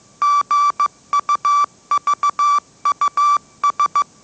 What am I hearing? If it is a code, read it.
Morse code, heard as a keyed tone, about 15 wpm, 1160 Hz. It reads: GUVUS
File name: mrx / mrx065 16wpm